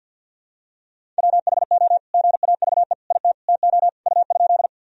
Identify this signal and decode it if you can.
Morse code, heard as a keyed tone, about 50 wpm, 700 Hz. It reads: WHO GAVE IT TO U?
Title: Sentences from Top 500 Words and Abbreviations 50wpm